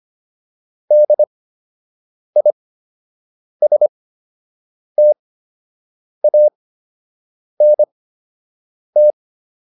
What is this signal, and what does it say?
Morse code, heard as a keyed tone, about 25 wpm, spaced as for 7 wpm, 600 Hz. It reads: DISTANT
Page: fundamental-practice-files